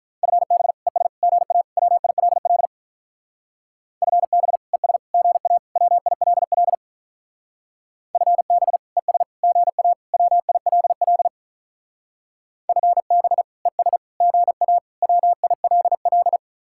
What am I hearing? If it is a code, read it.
Morse code, heard as a keyed tone, about 45 wpm, 700 Hz. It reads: FB ES GA WILL FB ES GA WILL FB ES GA WILL FB ES GA WILL